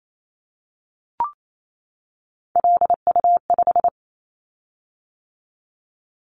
Morse code, heard as a keyed tone, about 28 wpm, 700 Hz. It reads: LU5
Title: Callsign Prefixes for Top 50 DXCC - 28wpm